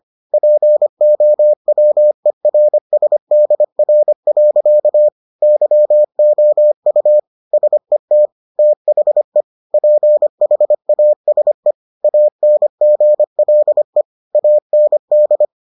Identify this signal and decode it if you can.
Morse code, heard as a keyed tone, about 25 wpm, 600 Hz. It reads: POWERSDR. YOU SET THE PHASE ANGLE AND